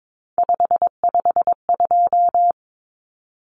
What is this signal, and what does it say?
Morse code, heard as a keyed tone, about 22 wpm, 700 Hz. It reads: 552
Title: Numbers - 3-Digit 22wpm